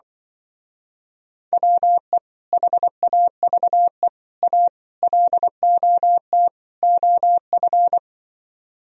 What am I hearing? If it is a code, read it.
Morse code, heard as a keyed tone, about 24 wpm, 700 Hz. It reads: WE HAVE A LOT OF
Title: Speed-Racing - Sets of 5 Words 20wpm